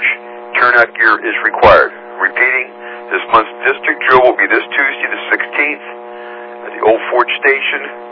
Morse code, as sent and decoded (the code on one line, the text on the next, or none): none